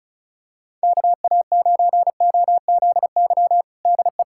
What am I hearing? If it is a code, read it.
Morse code, heard as a keyed tone, about 34 wpm, 700 Hz. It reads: KA9OZY DE
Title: USA Callsign DE Callsign - 6 Characters - 35wpm